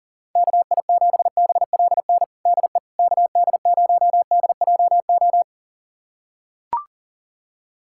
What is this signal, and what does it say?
Morse code, heard as a keyed tone, about 40 wpm, 700 Hz. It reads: KI7BLN DE KD0DJO
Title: USA Callsign DE Callsign 40wpm